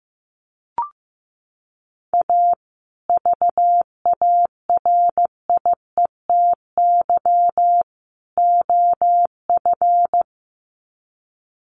Morse code, heard as a keyed tone, about 15 wpm, 700 Hz. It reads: A VARIETY OF
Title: Sets of 3 Words 15wpm